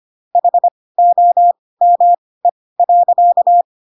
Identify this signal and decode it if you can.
Morse code, heard as a keyed tone, about 25 wpm, spaced as for 18 wpm, 700 Hz. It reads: HOME.
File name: Pinocchio ch3 12 25 ik1hge